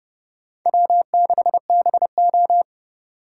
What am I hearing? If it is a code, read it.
Morse code, heard as a keyed tone, about 38 wpm, 700 Hz. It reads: W6BO